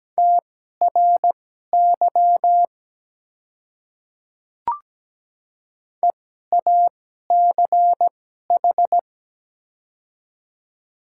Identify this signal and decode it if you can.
Morse code, heard as a keyed tone, about 17 wpm, spaced as for 12 wpm, 700 Hz. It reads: TRY EACH
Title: 2x Character Spacing - Top 200 Words - 17wpm